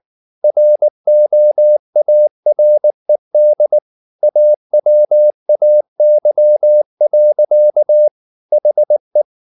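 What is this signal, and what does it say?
Morse code, heard as a keyed tone, about 19 wpm, 600 Hz. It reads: ROARED AWAY. HE
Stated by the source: Morse code at 19 wpm from CW med Gnister, reading Ken Follett's novel Never in English.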